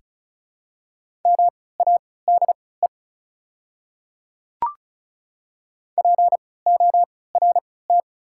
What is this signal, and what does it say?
Morse code, heard as a keyed tone, about 36 wpm, spaced as for 20 wpm, 690 Hz. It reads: MADE PORT
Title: 3x Character Spacing - Top 500 Words - 35wpm